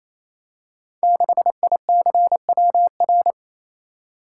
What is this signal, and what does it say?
Morse code, heard as a keyed tone, about 27 wpm, 700 Hz. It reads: 6ICWR
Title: Code Groups - 5 Letter:Numbers - 28wpm